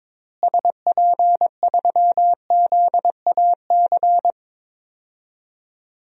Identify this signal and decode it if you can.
Morse code, heard as a keyed tone, about 22 wpm, 700 Hz. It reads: SP3ZAC
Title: DXCC-100 Call Signs - 6 Characters - Repeated 3x - 22wpm